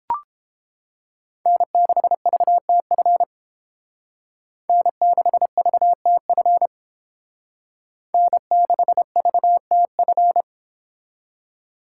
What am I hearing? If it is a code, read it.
Morse code, heard as a keyed tone, about 29 wpm, 700 Hz. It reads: N6VTF N6VTF N6VTF